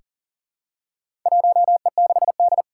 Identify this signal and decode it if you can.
Morse code, heard as a keyed tone, about 39 wpm, 700 Hz. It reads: 1E6D
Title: Four Challenging Letter-Number Combinations 40wpm